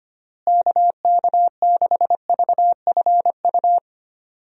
Morse code, heard as a keyed tone, about 25 wpm, 700 Hz. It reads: KK6VFU
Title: USA Call Signs - 6 Characters - Repeated 3x - 25wpm